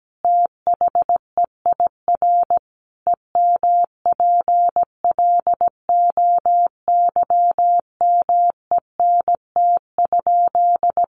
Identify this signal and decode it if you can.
Morse code, heard as a keyed tone, about 17 wpm, 700 Hz. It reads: THEIR EMPLOYMENT?
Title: Sentences from Top 1500 Words 17wpm